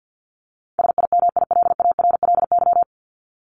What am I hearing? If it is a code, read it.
Morse code, heard as a keyed tone, about 50 wpm, 700 Hz. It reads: SIMILARLY
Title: Top 1251-1500 Words 50wpm